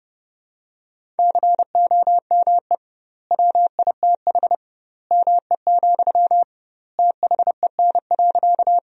Morse code, heard as a keyed tone, about 32 wpm, 700 Hz. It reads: COME WITH ME, THEN.
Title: 2x Word-Spacing - Sentences from Top 200 Words - 30wpm